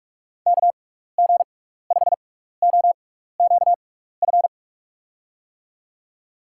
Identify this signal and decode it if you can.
Morse code, heard as a keyed tone, about 44 wpm, spaced as for 15 wpm, 700 Hz. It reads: KG5OQF
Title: USA Call Signs - 6 Characters - 6x Character Spacing - 45wpm